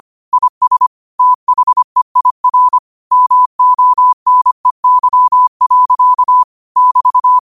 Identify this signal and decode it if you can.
Morse code, heard as a keyed tone, about 25 wpm, 1000 Hz. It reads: IS THEIR MONEY. <BT>
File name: qotdfiles / 25wpm